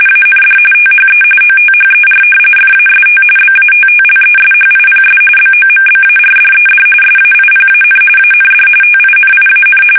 RTTY, about 45.45 baud. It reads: AND THE OTHER FAST BATTLESHIPS BOMBARDED SAIPAN AND TINIAN
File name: fsk850 / battleshipnj